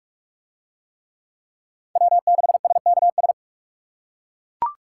Morse code, heard as a keyed tone, about 41 wpm, 700 Hz. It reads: W6SKS